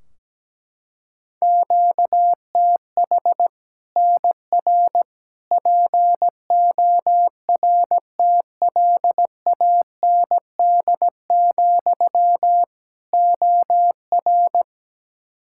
Morse code, heard as a keyed tone, about 17 wpm, 700 Hz. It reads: QTH NR PORTLAND, OR
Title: Practice QSO - Exchange 1 - Beginning - 17wpm